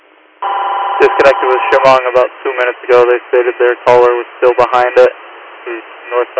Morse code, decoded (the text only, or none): none